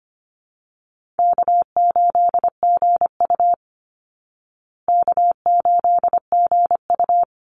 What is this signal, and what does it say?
Morse code, heard as a keyed tone, about 25 wpm, 700 Hz. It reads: K8GU K8GU
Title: Easy Call Signs Repeated 2x - 25wpm